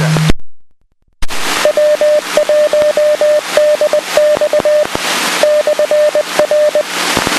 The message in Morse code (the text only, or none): W1DX /R